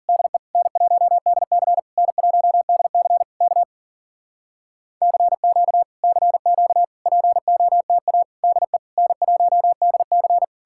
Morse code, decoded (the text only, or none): DE N1DC N1DC K CQ CQ POTA DE N1DC